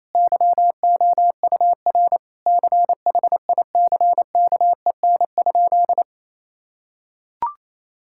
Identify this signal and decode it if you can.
Morse code, heard as a keyed tone, about 28 wpm, 700 Hz. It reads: YOUR CHICKEN?
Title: Sentences from Top 1000 Words - Review Part 5 - 28wpm